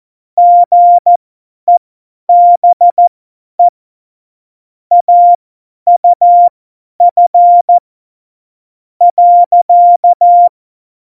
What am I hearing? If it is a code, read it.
Morse code, heard as a keyed tone, about 14 wpm, spaced as for 10 wpm, 700 Hz. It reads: GEBE AUF .
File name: Morsekurs / Witze / Witz 10 April 2018, 50 F70